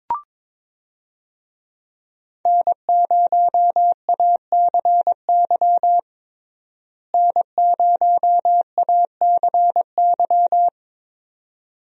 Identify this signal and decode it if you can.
Morse code, heard as a keyed tone, about 22 wpm, 700 Hz. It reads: N0ACY N0ACY
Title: USA Call Signs - 5 Characters - Repeated 2x - 22wpm